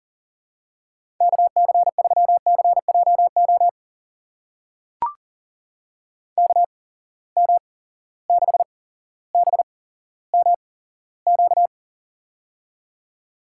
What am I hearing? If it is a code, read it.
Morse code, heard as a keyed tone, about 37 wpm, spaced as for 17 wpm, 700 Hz. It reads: KC3CJO KM6BMQ
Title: USA Call Signs - 6 Characters - 8x Character Spacing - 40wpm